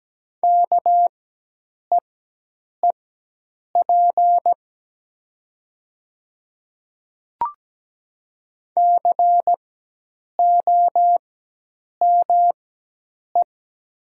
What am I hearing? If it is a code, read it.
Morse code, heard as a keyed tone, about 17 wpm, spaced as for 8 wpm, 700 Hz. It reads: KEEP COME
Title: 4x Character Spacing - Top 300 Words - 17wpm